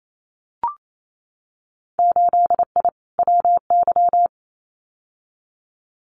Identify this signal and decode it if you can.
Morse code, heard as a keyed tone, about 28 wpm, 700 Hz. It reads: 8I WY
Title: Winter Field Day Exchanges Encore 28wpm